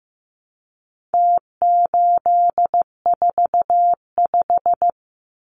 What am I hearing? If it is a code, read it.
Morse code, heard as a keyed tone, about 15 wpm, 700 Hz. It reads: T845
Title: Numbers - Contest Style - 4-Digit 15wpm